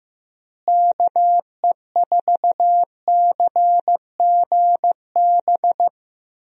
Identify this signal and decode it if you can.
Morse code, heard as a keyed tone, about 14 wpm, 700 Hz. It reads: KE4CGB
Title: USA Call Signs - 6 Characters - Repeated 3x - 15wpm